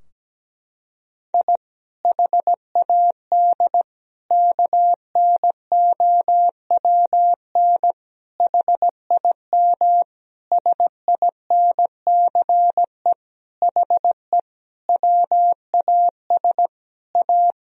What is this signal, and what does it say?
Morse code, heard as a keyed tone, about 17 wpm, 700 Hz. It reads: I HAD KNOWN HIM SINCE HE WAS A
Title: Sentences from Top 1000 Words - Review Part 3 - 17wpm